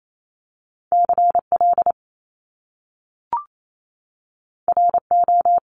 Morse code, heard as a keyed tone, about 28 wpm, 700 Hz. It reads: CL RO
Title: Top Two Letter Combinations in Words 28wpm